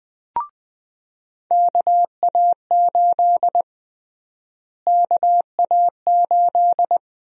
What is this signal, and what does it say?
Morse code, heard as a keyed tone, about 21 wpm, 700 Hz. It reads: KA8 KA8
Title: USA Call Sign Prefixes - Repeated 2x - 20wpm